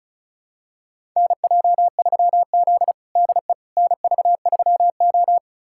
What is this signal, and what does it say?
Morse code, heard as a keyed tone, about 34 wpm, 700 Hz. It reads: NJ3Z DE NV3O
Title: USA Callsign DE Callsign - 4 Characters - 35wpm